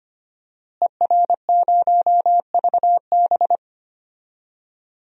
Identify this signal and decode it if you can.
Morse code, heard as a keyed tone, about 25 wpm, 700 Hz. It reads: ER0VB
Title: DXCC-100 Call Signs - 5 Characters - Repeated 3x - 25wpm